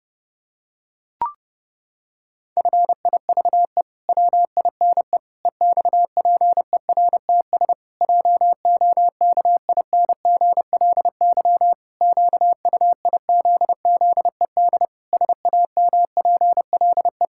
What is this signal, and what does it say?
Morse code, heard as a keyed tone, about 32 wpm, 700 Hz. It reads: FIVE WINE EXPERTS JOKINGLY QUIZZED SAMPLE
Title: Pangram Challenge v2 - 30wpm